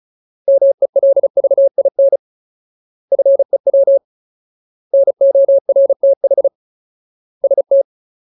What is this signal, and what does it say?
Morse code, heard as a keyed tone, about 34 wpm, 550 Hz. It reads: MELVIN FEW NORTH ST
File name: Ultimate Word List / 35WPM Ultimate